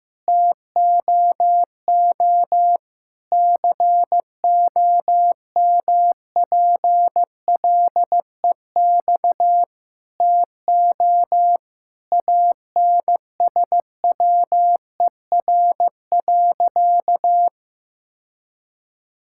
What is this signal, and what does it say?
Morse code, heard as a keyed tone, about 15 wpm, 700 Hz. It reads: TOO COMPLEX TO ANSWER.
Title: Sentences from Top 1000 Words - Review Part 6 - 15wpm